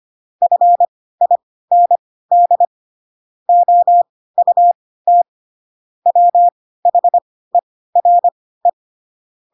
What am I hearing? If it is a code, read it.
Morse code, heard as a keyed tone, about 25 wpm, spaced as for 16 wpm, 700 Hz. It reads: FIND OUT WHERE
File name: Pinocchio / Pinocchio ch1 10 25 ik1hge